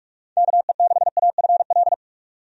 Morse code, heard as a keyed tone, about 46 wpm, 700 Hz. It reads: KE6AFL
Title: USA Call Signs - 6 Characters - 45wpm